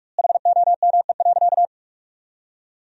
Morse code, heard as a keyed tone, about 47 wpm, 700 Hz. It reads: HOME.
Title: 2x Word-Spacing - Sentences from Top 500 Words - 45wpm